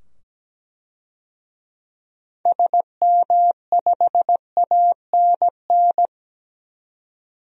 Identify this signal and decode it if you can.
Morse code, heard as a keyed tone, about 17 wpm, 700 Hz. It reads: SM5ANN
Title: DXCC-100 Call Signs - 6 Characters - 17wpm